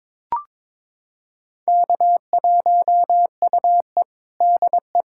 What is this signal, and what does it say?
Morse code, heard as a keyed tone, about 22 wpm, 700 Hz. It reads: K1UE DE
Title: USA Callsign DE Callsign - 4 Characters - 22wpm